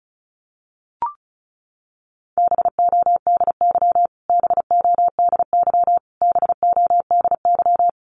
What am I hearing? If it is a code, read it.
Morse code, heard as a keyed tone, about 34 wpm, 700 Hz. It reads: BODY BODY BODY
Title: Top 500 Words Repeated 35wpm